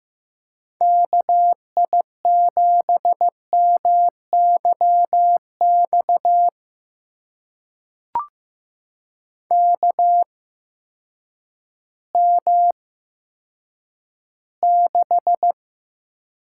Morse code, heard as a keyed tone, about 15 wpm, spaced as for 9 wpm, 700 Hz. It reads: KI7MYX KM6
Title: USA Call Signs - 6 Characters - 8x Character Spacing - 15wpm